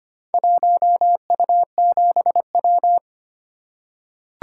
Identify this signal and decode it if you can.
Morse code, heard as a keyed tone, about 25 wpm, 700 Hz. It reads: 1U7W
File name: practice / 25wpm / Four Letter-Number Combinations - 25wpm